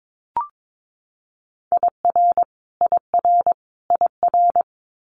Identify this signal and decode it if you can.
Morse code, heard as a keyed tone, about 22 wpm, 700 Hz. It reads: IR IR IR
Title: Multi-letter Phonemes Repeated 3x 22wpm